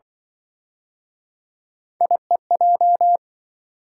Morse code, heard as a keyed tone, about 28 wpm, 700 Hz. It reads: IEJ